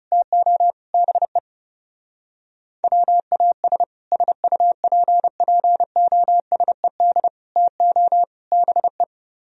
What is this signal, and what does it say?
Morse code, heard as a keyed tone, about 34 wpm, 700 Hz. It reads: TO BE WAS SUPPOSED TO BE
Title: Speed-Racing - Sets of 4 Words 30wpm